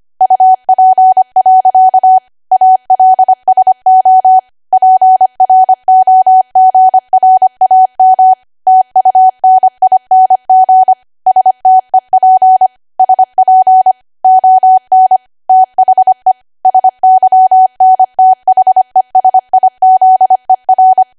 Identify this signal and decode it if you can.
Morse code, heard as a keyed tone, about 25 wpm, 750 Hz. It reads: UP. ALSO PROGRAM TUNING STEP SP ON THE SYNTHESIZER
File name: Archive / 25 WPM / 25WPM